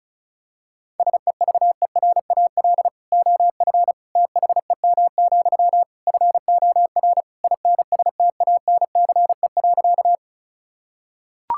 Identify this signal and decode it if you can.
Morse code, heard as a keyed tone, about 36 wpm, 700 Hz. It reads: SEVERAL OF THEM, FOR INSTANCE.
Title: Sentences from Top 1000 Words - Review Part 5 - 35wpm